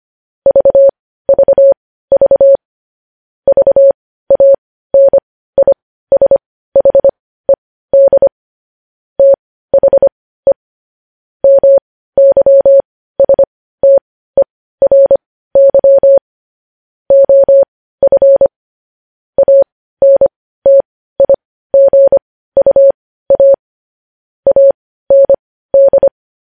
Morse code, heard as a keyed tone, about 25 wpm, spaced as for 15 wpm, 550 Hz. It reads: VVV VANISHED THE MYSTERY OF ANTIGUA AND
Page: News Headlines in Morse Code at 15 WPM